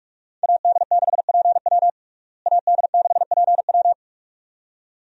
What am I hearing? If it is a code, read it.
Morse code, heard as a keyed tone, about 41 wpm, 700 Hz. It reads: AD6PW AD6PW
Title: USA Call Signs - 5 Characters - Repeated 2x - 45wpm